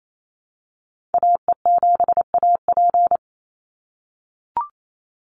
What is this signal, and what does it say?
Morse code, heard as a keyed tone, about 28 wpm, 700 Hz. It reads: AE7AP
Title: NA SOTA Activator Callsigns 28wpm